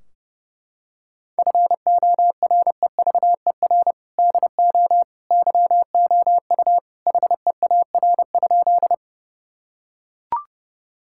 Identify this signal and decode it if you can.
Morse code, heard as a keyed tone, about 30 wpm, 700 Hz. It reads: FOREVER DO YOU HEAR?
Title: Sentences from Top 1600 Words 30wpm